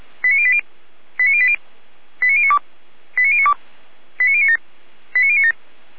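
Five-tone selective call (ZVEI), heard as quick runs of five tones, tones of about 70 ms. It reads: 89080 89080 89092 89092 89097 89097
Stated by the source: Five-tone ZVEI selective call 89073, received by SDR, from Feuerwehr Orlamünde.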